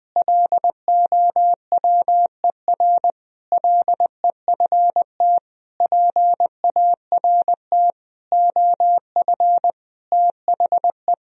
Morse code, heard as a keyed tone, about 21 wpm, 700 Hz. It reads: LOWER LEFT PART OF THE
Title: Sentences from Top 1000 Words - Review Part 4 - 20wpm